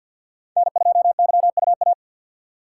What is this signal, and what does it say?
Morse code, heard as a keyed tone, about 53 wpm, 700 Hz. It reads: N2YUA
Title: USA Call Signs - 5 Characters - 4x Character Spacing - 50wpm